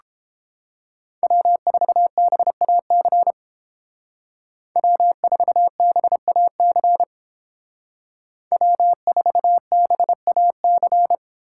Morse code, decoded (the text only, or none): W4BAC W4BAC W4BAC